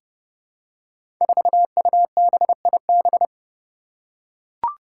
Morse code, heard as a keyed tone, about 30 wpm, 700 Hz. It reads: 4UBIB
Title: Five Challenging Letter-Number Combinations 30wpm